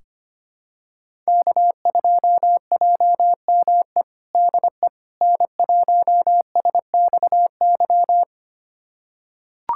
Morse code, heard as a keyed tone, about 25 wpm, 700 Hz. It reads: K2JME DE N1SXY
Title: USA Callsign DE Callsign - 5 Characters - 25wpm